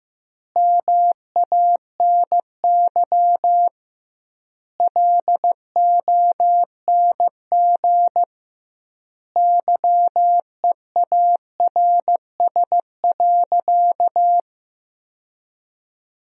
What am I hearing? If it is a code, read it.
Morse code, heard as a keyed tone, about 16 wpm, 700 Hz. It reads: MANY LONG YEARS.
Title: 2x Word-Spacing - Sentences from Top 200 Words - 15wpm